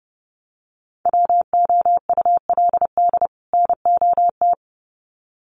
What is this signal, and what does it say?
Morse code, heard as a keyed tone, about 30 wpm, 700 Hz. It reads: WOULD NOT
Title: Sets of 2 Words 30wpm